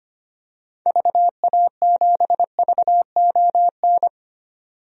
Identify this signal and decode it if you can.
Morse code, heard as a keyed tone, about 25 wpm, 700 Hz. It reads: VA7VON
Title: DXCC-100 Call Signs - 6 Characters - 25wpm